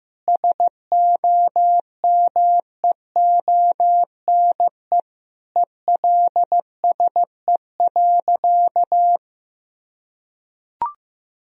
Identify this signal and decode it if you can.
Morse code, heard as a keyed tone, about 15 wpm, 700 Hz. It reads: SOMEONE ELSE.
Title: Sentences from Top 1000 Words - Review Part 4 - 15wpm